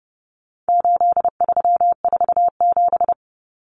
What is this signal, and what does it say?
Morse code, heard as a keyed tone, about 28 wpm, 700 Hz. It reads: 8347
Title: Numbers - 4-Digit 30wpm